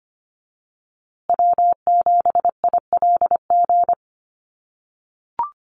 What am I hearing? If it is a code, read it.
Morse code, heard as a keyed tone, about 25 wpm, 700 Hz. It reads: W7ILG
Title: USA Call Signs Repeated 3x 25wpm